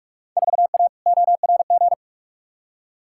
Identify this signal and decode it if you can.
Morse code, heard as a keyed tone, about 46 wpm, 700 Hz. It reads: 4A ORG
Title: Field Day Exchanges - 45wpm